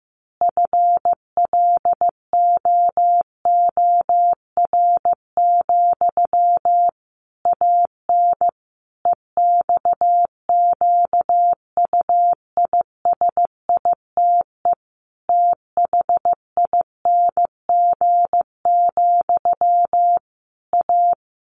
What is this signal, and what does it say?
Morse code, heard as a keyed tone, about 15 wpm, 700 Hz. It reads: FLOOR, AN EXQUISITE THING, A
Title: A Sound Of Thunder 15wpm